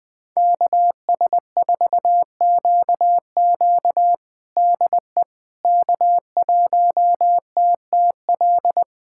Morse code, heard as a keyed tone, about 20 wpm, 700 Hz. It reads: KS4QQ DE K1TTL
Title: USA Callsign DE Callsign - 5 Characters - 20wpm